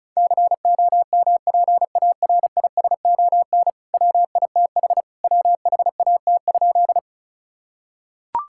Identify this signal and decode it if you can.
Morse code, heard as a keyed tone, about 36 wpm, 700 Hz. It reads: COMPARISON WITH WHAT?
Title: Sentences from Top 1600 Words 35wpm